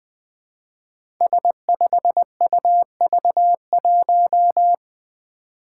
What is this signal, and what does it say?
Morse code, heard as a keyed tone, about 20 wpm, 700 Hz. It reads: S5UV1
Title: Five Challenging Letter-Number Combinations 20wpm